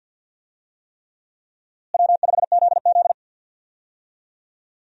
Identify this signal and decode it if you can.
Morse code, heard as a keyed tone, about 51 wpm, 700 Hz. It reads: W5ZZ
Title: USA Call Signs - 4 Characters - Repeated 3x - 50wpm